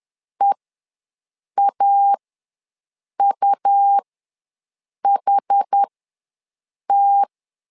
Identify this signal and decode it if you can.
Morse code, heard as a keyed tone, about 10 wpm, spaced as for 6 wpm, 800 Hz. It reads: EAUHT
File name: LEZIONI / 2 UHDN / 2 UHD ws 10wpm 2058